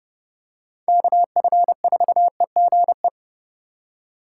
Morse code, heard as a keyed tone, about 30 wpm, 700 Hz. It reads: KF4EGE
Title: USA Call Signs - 6 Characters - 2x Character Spacing - 30wpm